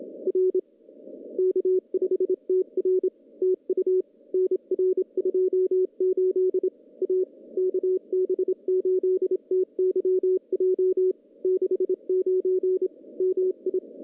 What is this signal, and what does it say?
Morse code, heard as a keyed tone, about 27 wpm, 410 Hz. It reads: R K5TR TU NR 28 A KB8TYJ 69 MI
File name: contests / CW-SS-k5tr- -0200